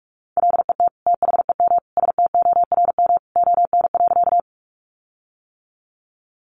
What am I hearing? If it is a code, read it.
Morse code, heard as a keyed tone, about 44 wpm, 700 Hz. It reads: LET THEM STORM ON.
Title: Sentences from Top 1500 Words 45wpm